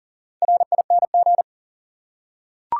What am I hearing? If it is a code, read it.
Morse code, heard as a keyed tone, about 40 wpm, 700 Hz. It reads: RING
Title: Four Letter Words - 40wpm